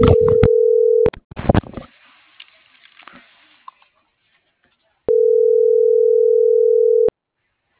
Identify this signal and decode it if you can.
Ringback tone (North America), 440 + 480 Hz.